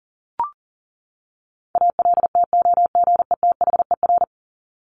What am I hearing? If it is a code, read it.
Morse code, heard as a keyed tone, about 38 wpm, 700 Hz. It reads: ALTOGETHER
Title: Top 1501-1750 Words 40wpm